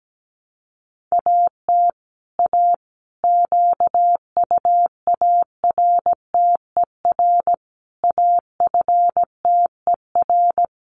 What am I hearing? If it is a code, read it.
Morse code, heard as a keyed tone, about 17 wpm, 700 Hz. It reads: AT A QUARTER AFTER
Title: Sentences from Top 1300 Words 17wpm